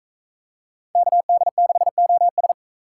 Morse code, heard as a keyed tone, about 40 wpm, 700 Hz. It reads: KD6OS